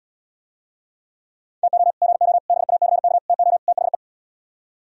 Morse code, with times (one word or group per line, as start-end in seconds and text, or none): 1.6-1.9 TO
2.0-2.4 GO
2.5-3.2 BACK
3.3-3.6 TO
3.7-3.9 THE